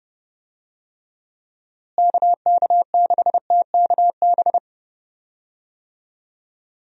Morse code, heard as a keyed tone, about 30 wpm, 700 Hz. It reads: KK6TKB
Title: USA Call Signs - 6 Characters - Repeated 3x - 30wpm